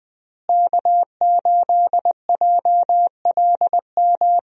Morse code, heard as a keyed tone, about 20 wpm, 700 Hz. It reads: K8JLM
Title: USA Call Signs - 5 Characters - Repeated 2x - 20wpm